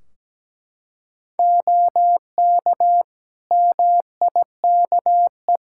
Morse code, heard as a keyed tone, about 17 wpm, 700 Hz. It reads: OK MIKE
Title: Practice QSO - Exchange 3 - More Info - 17wpm